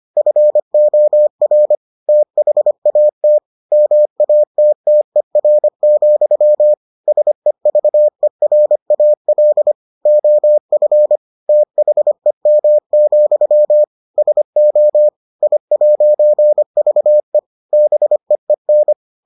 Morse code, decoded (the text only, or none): FOR THAT MATTER, SEVERAL OF THEM, SO I'VE BEEN